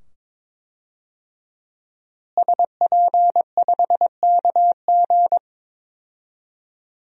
Morse code, heard as a keyed tone, about 22 wpm, 700 Hz. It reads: SP5KG
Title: DXCC-100 Call Signs - 5 Characters - Repeated 2x - 22wpm